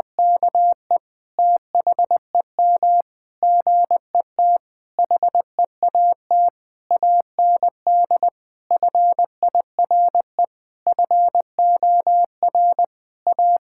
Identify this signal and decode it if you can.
Morse code, heard as a keyed tone, about 20 wpm, 700 Hz. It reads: KE THEM GET HEAT AND FIRE FOR A
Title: Sentences from Top 1600 Words 20wpm